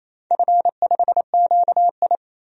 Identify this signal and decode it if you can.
Morse code, heard as a keyed tone, about 28 wpm, 700 Hz. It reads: F5QI
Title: DXCC-100 Call Signs - 4 Characters - 28wpm